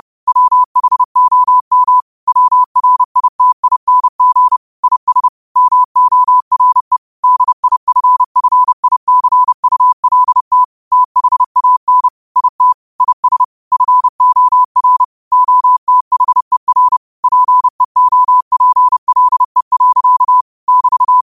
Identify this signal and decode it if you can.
Morse code, heard as a keyed tone, about 31 wpm, 1000 Hz. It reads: WHOM WRITING IS MORE DIFFICULT THAN IT IS FOR OTHER PEOPLE. <BT>
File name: qotdfiles / 30wpm